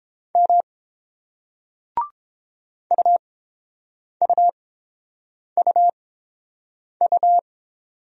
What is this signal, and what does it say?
Morse code, heard as a keyed tone, about 29 wpm, spaced as for 7 wpm, 700 Hz. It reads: M UUUU